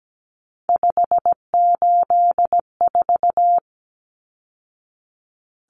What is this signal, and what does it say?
Morse code, heard as a keyed tone, about 17 wpm, 700 Hz. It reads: 584
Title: Numbers - 3-Digit 17wpm